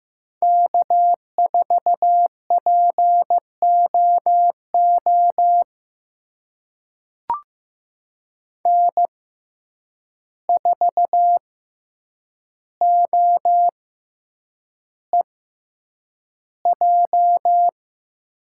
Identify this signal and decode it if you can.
Morse code, heard as a keyed tone, about 15 wpm, spaced as for 8 wpm, 700 Hz. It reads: K4POO N4OEJ
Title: USA Call Signs - 5 Characters - 6x Character Spacing - 15wpm